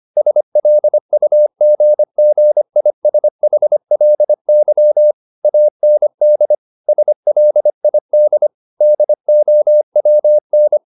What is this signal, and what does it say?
Morse code, heard as a keyed tone, about 25 wpm, 600 Hz. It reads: SLUGGISHLY AND SLID DOWN